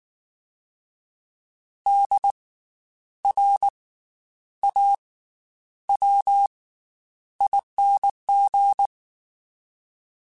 Morse code, heard as a keyed tone, about 19 wpm, spaced as for 9 wpm, 800 Hz. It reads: DRAWING
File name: Special INGs / Special INGs-mp3s / Slow ing @18